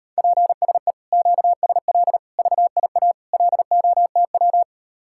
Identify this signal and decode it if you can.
Morse code, heard as a keyed tone, about 38 wpm, 700 Hz. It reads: PSE QSL VIA LOTW